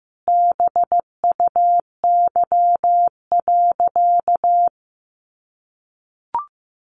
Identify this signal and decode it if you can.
Morse code, heard as a keyed tone, about 15 wpm, 700 Hz. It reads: BUY.
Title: Sentences from Top 1300 Words 15wpm